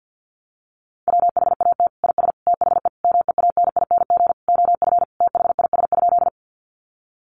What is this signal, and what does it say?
Morse code, heard as a keyed tone, about 49 wpm, 700 Hz. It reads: WHAT IS THE MEANING OF THIS?
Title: Sentences from Top 900 Words 50wpm